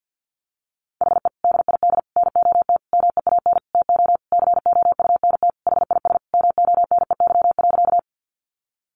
Morse code, heard as a keyed tone, about 50 wpm, 700 Hz. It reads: HE DID NOT MEAN TO COUNT HIS MONEY.
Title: Sentences from Top 1500 Words 50wpm